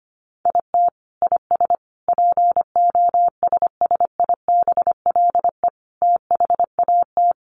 Morse code, heard as a keyed tone, about 25 wpm, 700 Hz. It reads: IT IS POSSIBLE THAT
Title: Sets of 5 Words Encore2 25wpm